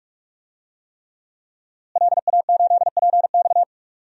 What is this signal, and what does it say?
Morse code, heard as a keyed tone, about 46 wpm, 700 Hz. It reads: LA8PX
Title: DXCC-100 Call Signs - 5 Characters - Repeated 3x - 45wpm